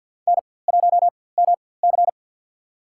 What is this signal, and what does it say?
Morse code, heard as a keyed tone, about 48 wpm, spaced as for 23 wpm, 700 Hz. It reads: N1MC
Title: USA Call Signs - 4x Character Spacing - 50wpm